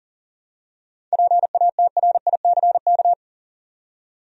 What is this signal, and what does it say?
Morse code, heard as a keyed tone, about 40 wpm, 700 Hz. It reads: PATRICK
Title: USA Common Names - 40wpm